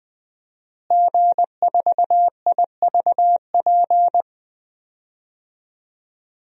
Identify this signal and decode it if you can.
Morse code, heard as a keyed tone, about 20 wpm, 700 Hz. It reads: G4IVP
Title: DXCC-100 Call Signs - 5 Characters - Repeated 3x - 20wpm